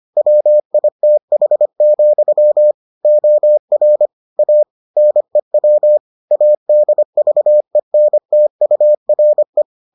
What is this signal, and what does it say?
Morse code, heard as a keyed tone, about 25 wpm, 600 Hz. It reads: WITH, OR A NEW ADVENTURE